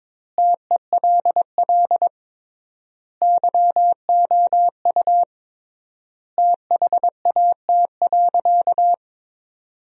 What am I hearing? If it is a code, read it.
Morse code, heard as a keyed tone, about 22 wpm, 700 Hz. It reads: TELL YOU THAT.
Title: 3x Word-Spacing - Sentences from Top 300 Words - 22wpm